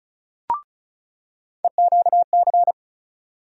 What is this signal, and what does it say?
Morse code, heard as a keyed tone, about 36 wpm, 700 Hz. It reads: EQC
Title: USA Call Sign Postfixes - 3 Letters 35wpm